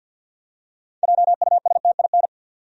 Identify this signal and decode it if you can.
Morse code, heard as a keyed tone, about 47 wpm, 700 Hz. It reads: JUSTIN
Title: USA Common Names - 50wpm